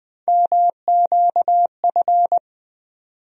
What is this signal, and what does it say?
Morse code, heard as a keyed tone, about 20 wpm, 700 Hz. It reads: MQF
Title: USA Call Sign Postfixes - 3 Letters 20wpm